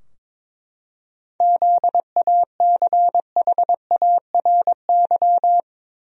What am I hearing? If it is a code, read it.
Morse code, heard as a keyed tone, about 22 wpm, 700 Hz. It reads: ZACHARY
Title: USA Common Names - 22wpm